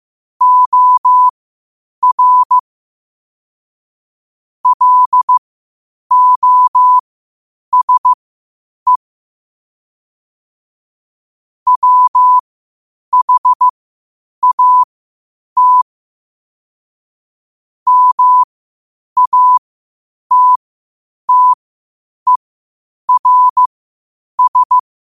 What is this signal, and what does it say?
Morse code, heard as a keyed tone, about 15 wpm, spaced as for 8 wpm, 1000 Hz. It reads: OR LOSE WHAT MATTERS
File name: qotdfiles / 8wpm